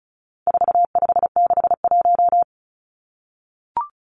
Morse code, heard as a keyed tone, about 34 wpm, 700 Hz. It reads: 4561
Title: Numbers - 4-Digit 35wpm